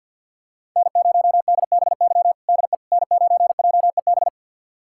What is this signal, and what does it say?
Morse code, heard as a keyed tone, about 51 wpm, 700 Hz. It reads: N0DBY DE N9JEB
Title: USA Callsign DE Callsign 50wpm